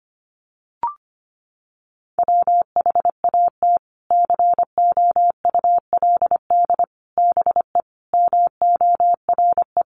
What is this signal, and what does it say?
Morse code, heard as a keyed tone, about 25 wpm, 700 Hz. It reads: WHAT COULD BE MORE
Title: Sentences from Top 1000 Words 25wpm